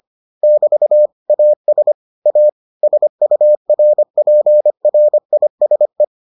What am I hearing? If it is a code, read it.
Morse code, heard as a keyed tone, about 25 wpm, 600 Hz. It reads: <BT> AS A SURPRISE